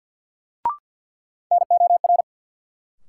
Morse code, heard as a keyed tone, about 52 wpm, 700 Hz. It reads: NOR
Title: Sentences from Top 1500 Words - New Word Drill Alt - 50wpm